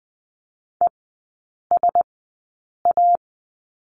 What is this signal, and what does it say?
Morse code, heard as a keyed tone, about 20 wpm, spaced as for 8 wpm, 700 Hz. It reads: ESA
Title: 074 Lesson 13e - Letter L - Three Characters 20wpm